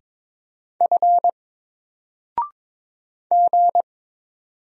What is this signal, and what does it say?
Morse code, heard as a keyed tone, about 22 wpm, 700 Hz. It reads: F G
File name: Single-Letter-Number-22wpm